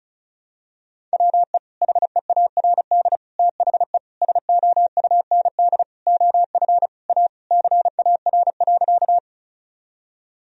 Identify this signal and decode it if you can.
Morse code, heard as a keyed tone, about 34 wpm, 700 Hz. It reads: WE HEARD THE SOUND OF A CAR.
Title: Sentences from Top 1000 Words - Review Part 1 - 35wpm